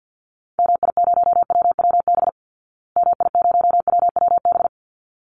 Morse code, heard as a keyed tone, about 50 wpm, 700 Hz. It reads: MI0WWB MI0WWB
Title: Call Signs Repeated 3x - 50wpm